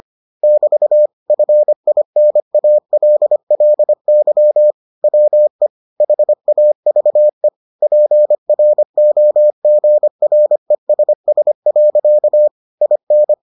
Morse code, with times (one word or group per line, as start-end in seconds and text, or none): 0.4-1.1 <BT>
1.3-4.7 FINALLY
5.0-5.7 WE
6.0-7.5 HAVE
7.8-12.5 PROGRESS.
12.8-13.3 IN